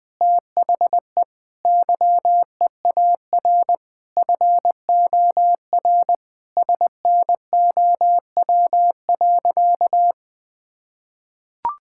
Morse code, heard as a keyed tone, about 20 wpm, 700 Hz. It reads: THE YEAR FOR SNOW.
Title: Sentences with 4-Word Phrases - 20wpm